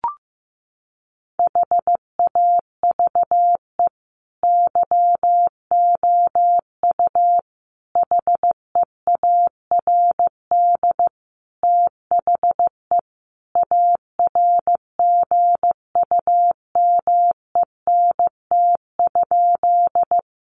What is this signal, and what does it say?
Morse code, heard as a keyed tone, about 15 wpm, 700 Hz. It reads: HAVE YOU HEARD THE ARGUMENT?
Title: Sentences from Top 1400 Words 15wpm